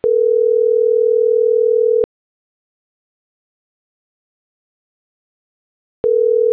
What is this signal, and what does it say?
Ringback tone (North America), 440 + 480 Hz.